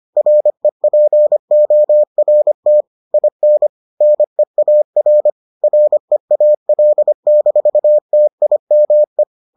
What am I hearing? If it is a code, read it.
Morse code, heard as a keyed tone, about 25 wpm, 600 Hz. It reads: REPORT IN NEAR REAL-TIME